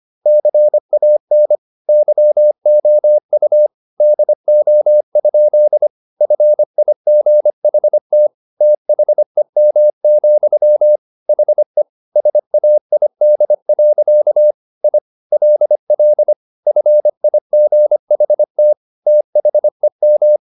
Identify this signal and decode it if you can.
Morse code, heard as a keyed tone, about 25 wpm, 600 Hz. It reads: CAN YOU DO? FIGHT THEM, HE SAID. I LL FIGHT THEM